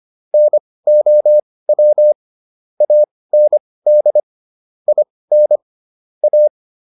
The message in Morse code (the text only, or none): NOW AND IN A